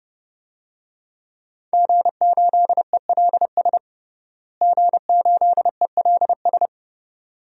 Morse code, heard as a keyed tone, about 30 wpm, 700 Hz. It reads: G8ELS G8ELS
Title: DXCC-100 Call Signs - 5 Characters - Repeated 2x - 30wpm